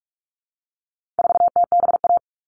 Morse code, heard as a keyed tone, about 44 wpm, 700 Hz. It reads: 4TBA
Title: All Cut Numbers - 4-Digit 45wpm